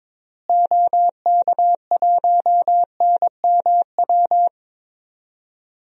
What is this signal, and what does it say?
Morse code, heard as a keyed tone, about 22 wpm, 700 Hz. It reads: OK1NMW
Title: DXCC-100 Call Signs - 6 Characters - Repeated 3x - 22wpm